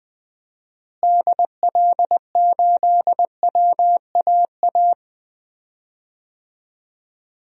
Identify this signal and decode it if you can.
Morse code, heard as a keyed tone, about 20 wpm, 700 Hz. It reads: DL8WAA
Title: DXCC-100 Call Signs - 6 Characters - Repeated 2x - 20wpm